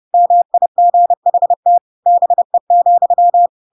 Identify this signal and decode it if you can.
Morse code, heard as a keyed tone, about 30 wpm, 700 Hz. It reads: MIGHT BE,
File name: Pinocchio / Pinocchio ch5 30 30 ik1hge